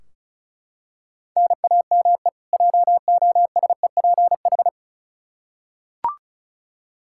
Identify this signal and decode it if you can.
Morse code, heard as a keyed tone, about 34 wpm, 700 Hz. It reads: NAME JOSEPH
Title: K1USN SST QSO Practice 35wpm